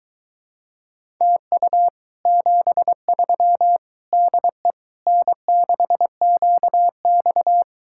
Morse code, heard as a keyed tone, about 22 wpm, 700 Hz. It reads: TU 73 DE N6QX